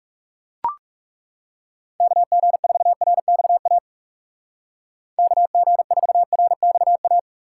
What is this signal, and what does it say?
Morse code, heard as a keyed tone, about 36 wpm, 700 Hz. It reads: KG4RXA KG4RXA
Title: Speed Racing - USA Call Signs 30wpm